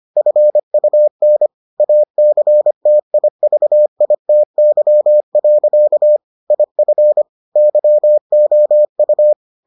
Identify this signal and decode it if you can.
Morse code, heard as a keyed tone, about 25 wpm, 600 Hz. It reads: FUN ACTIVITY. IF YOU